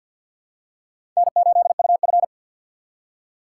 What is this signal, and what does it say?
Morse code, heard as a keyed tone, about 51 wpm, 700 Hz. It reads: N8UF
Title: USA Call Signs - 4 Characters - 3x Character Spacing - 50wpm